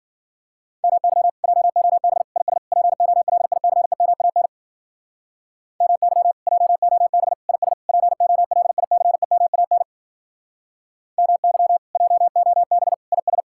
Morse code, heard as a keyed tone, about 51 wpm, 700 Hz. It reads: MY JOB IS POLICEMAN MY JOB IS POLICEMAN MY JOB IS